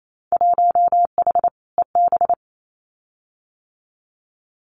Morse code, heard as a keyed tone, about 28 wpm, 700 Hz. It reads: 1H EB
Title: Winter Field Day Exchanges 28wpm